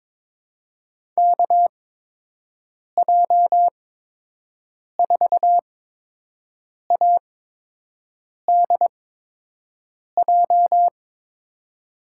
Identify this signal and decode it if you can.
Morse code, heard as a keyed tone, about 21 wpm, spaced as for 6 wpm, 700 Hz. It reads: KJ4ADJ
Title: USA Call Signs - 6 Characters - 8x Character Spacing - 22wpm